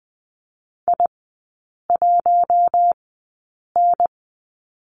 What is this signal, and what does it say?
Morse code, heard as a keyed tone, about 20 wpm, spaced as for 8 wpm, 700 Hz. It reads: I1N
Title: 041 Lesson 8e - Number 1 - Three Characters 20wpm